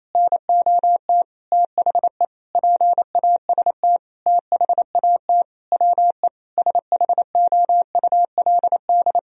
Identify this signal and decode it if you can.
Morse code, heard as a keyed tone, about 28 wpm, 700 Hz. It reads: NOT THE PAST THAT WE SHOULD
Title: Sentences from Top 1000 Words - Review Part 4 - 28wpm